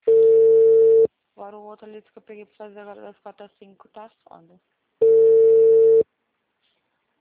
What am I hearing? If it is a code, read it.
Ringback tone (China, 450 Hz), with speech.
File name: out- -2006- - - .1297